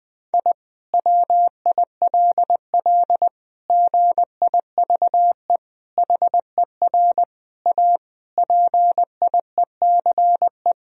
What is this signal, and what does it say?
Morse code, heard as a keyed tone, about 20 wpm, 700 Hz. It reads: I WILL GIVE HER A PIECE
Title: Sentences from Top 1000 Words - Review Part 2 - 20wpm